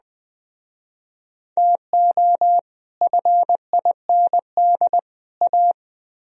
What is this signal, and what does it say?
Morse code, heard as a keyed tone, about 20 wpm, 700 Hz. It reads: TO FIND A
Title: Speed-Racing - Sets of 3 Words 17wpm